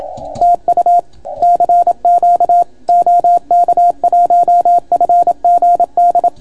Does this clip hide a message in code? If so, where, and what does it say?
Yes: Morse code from 0.4 to 6.3 s, heard as a keyed tone, about 28 wpm, 700 Hz. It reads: TU CQ OK1FGD